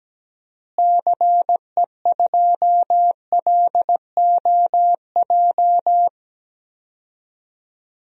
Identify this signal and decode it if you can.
Morse code, heard as a keyed tone, about 17 wpm, 700 Hz. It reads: CE2LOJ
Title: DXCC-100 Call Signs - 6 Characters - Repeated 3x - 17wpm